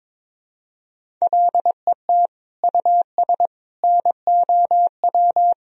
Morse code, heard as a keyed tone, about 21 wpm, 700 Hz. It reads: LET US NOW
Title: Sentences from Top 1000 Words - Review Part 1 - 22wpm